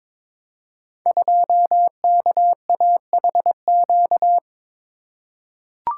Morse code, heard as a keyed tone, about 22 wpm, 700 Hz. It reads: 2KAHQ
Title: Code Groups - 5 Letter:Numbers - 22wpm